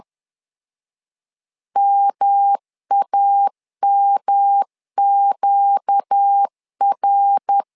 Morse code, heard as a keyed tone, about 10 wpm, 800 Hz. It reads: MAMQR
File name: LEZIONI / 4 Q 10wpm 2210